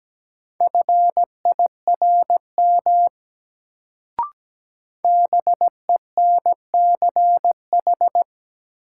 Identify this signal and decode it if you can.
Morse code, heard as a keyed tone, about 17 wpm, 700 Hz. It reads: FIRM BENCH
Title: Sentences from Top 1400 Words - New Word Drill - 17wpm